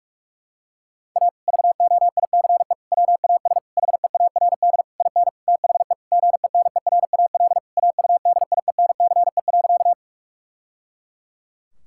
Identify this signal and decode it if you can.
Morse code, heard as a keyed tone, about 42 wpm, 700 Hz. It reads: A VOICE WAS HEARD IN THE GENERAL AUDIENCE.
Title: Sentences from Top 1000 Words - Review Part 5 - 45wpm